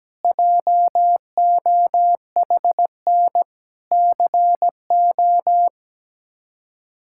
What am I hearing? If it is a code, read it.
Morse code, heard as a keyed tone, about 17 wpm, 700 Hz. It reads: JOHN CO
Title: K1USN SST QSO Practice - 17wpm